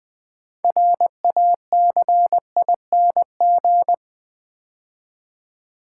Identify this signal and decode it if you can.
Morse code, heard as a keyed tone, about 20 wpm, 700 Hz. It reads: RACING
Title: Build A Word - 2 Letters At A Time - 6-Letter Words - 20wpm